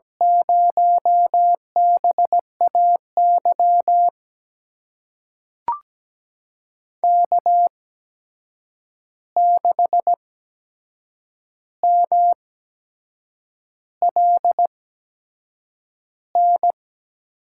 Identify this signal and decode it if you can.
Morse code, heard as a keyed tone, about 17 wpm, spaced as for 7 wpm, 700 Hz. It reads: 0BAY K6MLN
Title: USA Call Signs - 5 Characters - 8x Character Spacing - 17wpm